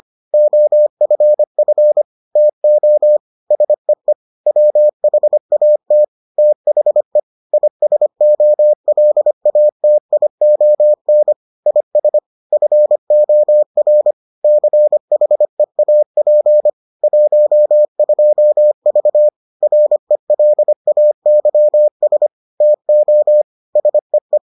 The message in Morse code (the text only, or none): OFF TO SEE WHAT THE ISOLATION IS FOR CHEAP 12V RELAYS TO SEE